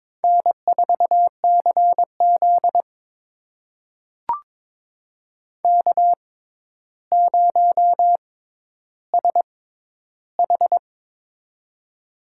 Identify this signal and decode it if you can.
Morse code, heard as a keyed tone, about 21 wpm, spaced as for 11 wpm, 700 Hz. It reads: N4CZ K0SH
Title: USA Call Signs - 4 Characters - 6x Character Spacing - 22wpm